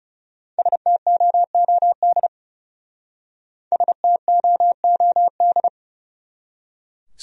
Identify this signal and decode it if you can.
Morse code, heard as a keyed tone, about 37 wpm, 700 Hz. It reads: STOOD STOOD
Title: Speed-Racing Top 500 Words 30wpm